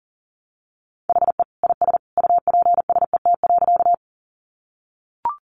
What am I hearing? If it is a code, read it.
Morse code, heard as a keyed tone, about 37 wpm, 700 Hz. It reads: HE IS UPSET.
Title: Additional Easy Phrases 40wpm